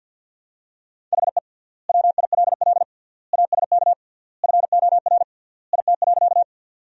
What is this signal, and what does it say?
Morse code, heard as a keyed tone, about 50 wpm, 700 Hz. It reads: HE WILL ASK FOR IT.
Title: 3x Word-Spacing - Sentences from Top 200 Words - 50wpm